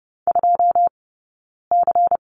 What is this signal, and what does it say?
Morse code, heard as a keyed tone, about 30 wpm, spaced as for 9 wpm, 690 Hz. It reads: 2C
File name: Lesson 18d - Letter m - Two Characters 30wpm